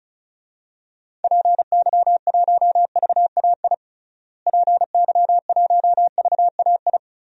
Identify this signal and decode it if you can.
Morse code, heard as a keyed tone, about 36 wpm, 700 Hz. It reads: PY1VAI PY1VAI
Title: DXCC-100 Call Signs - 6 Characters - Repeated 2x - 35wpm